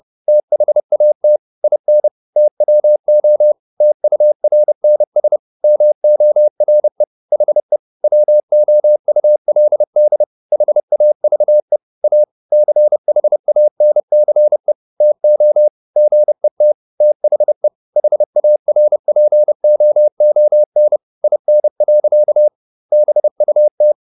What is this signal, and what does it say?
Morse code, heard as a keyed tone, about 30 wpm, 600 Hz. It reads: THAT IN TWO TURNS MORE HE WOULD HAVE A CHANCE TO GET THE HARPOON IN. BUT